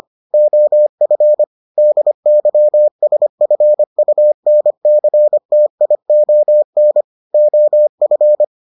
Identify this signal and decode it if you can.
Morse code, heard as a keyed tone, about 25 wpm, 600 Hz. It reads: OF DYSFUNCTION OF